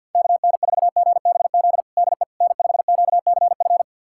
Morse code, heard as a keyed tone, about 49 wpm, 700 Hz. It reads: KN4GBZ DE N5QCF
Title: USA Callsign DE Callsign 50wpm